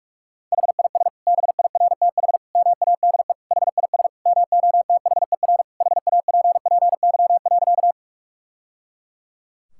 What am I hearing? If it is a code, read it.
Morse code, heard as a keyed tone, about 41 wpm, 700 Hz. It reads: HIS BIRTH MADE HIS MOTHER HAPPY.
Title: Sentences from Top 1000 Words - Review Part 5 - 45wpm